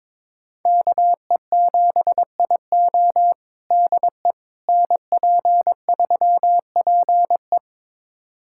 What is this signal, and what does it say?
Morse code, heard as a keyed tone, about 22 wpm, 700 Hz. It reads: KE7IO DE NP3PE
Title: USA Callsign DE Callsign - 5 Characters - 22wpm